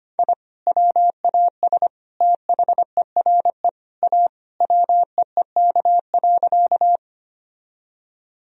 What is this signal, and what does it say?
Morse code, heard as a keyed tone, about 25 wpm, 700 Hz. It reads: I WAS THERE A WEEK.
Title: Sentences from Top 1000 Words - Review Part 2 - 25wpm